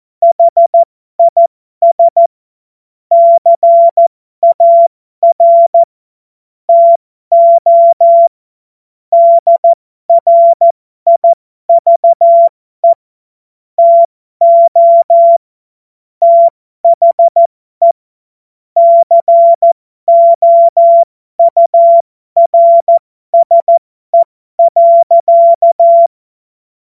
Morse code, heard as a keyed tone, about 14 wpm, spaced as for 12 wpm, 680 Hz. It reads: HIS CAR TO DRIVE TO THE COURSE.
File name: Morsekurs / Witze